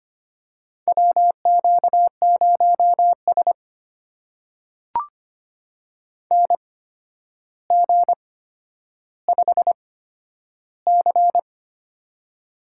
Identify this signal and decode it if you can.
Morse code, heard as a keyed tone, about 25 wpm, spaced as for 11 wpm, 700 Hz. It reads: WQ0S NG5C
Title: USA Call Signs - 4 Characters - 8x Character Spacing - 25wpm